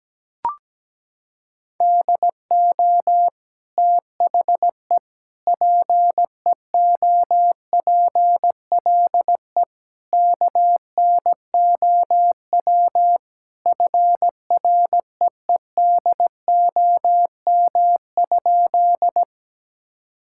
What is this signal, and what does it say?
Morse code, heard as a keyed tone, about 17 wpm, 700 Hz. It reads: DO THE PEOPLE KNOW FREEDOM?
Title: Sentences from Top 1000 Words - Review Part 5 - 17wpm